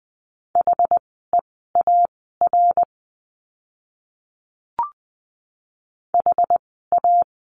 Morse code, heard as a keyed tone, about 20 wpm, spaced as for 14 wpm, 700 Hz. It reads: HEAR HA
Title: 061 Lesson 11f - Letter h - Words 20wpm